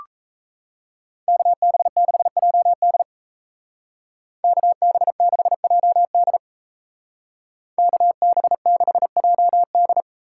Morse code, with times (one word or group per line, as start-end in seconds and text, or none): 1.3-3.0 KB6JD
4.4-6.4 KB6JD
7.8-10.0 KB6JD